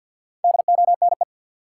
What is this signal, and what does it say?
Morse code, heard as a keyed tone, about 48 wpm, 700 Hz. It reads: DONE
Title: Sentences from Top 500 Words - New Word Drill - 50wpm